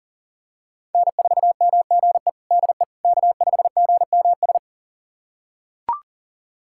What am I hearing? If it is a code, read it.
Morse code, heard as a keyed tone, about 40 wpm, 700 Hz. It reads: N4MGE DE K5GMS
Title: USA Callsign DE Callsign - 5 Characters - 40wpm